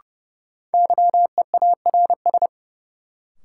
Morse code, heard as a keyed tone, about 30 wpm, 700 Hz. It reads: YEARS
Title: Sets of 2 Words - New Word Drill - 30wpm